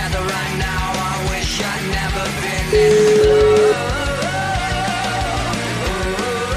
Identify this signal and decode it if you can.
Ringback tone (Europe, 425 Hz), with noise.